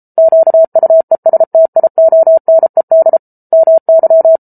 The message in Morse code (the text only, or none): QUESTIONED MY